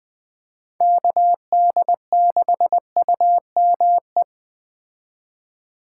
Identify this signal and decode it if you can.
Morse code, heard as a keyed tone, about 20 wpm, 700 Hz. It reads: KD6UME
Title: Speed Racing - USA 2x3 Call Signs 17wpm